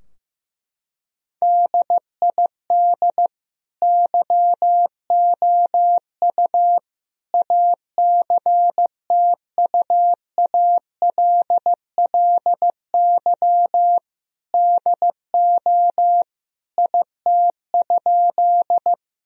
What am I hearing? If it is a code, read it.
Morse code, heard as a keyed tone, about 16 wpm, 700 Hz. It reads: DID YOU ACTUALLY DO IT?
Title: Sentences from Top 1000 Words - Review Part 4 - 15wpm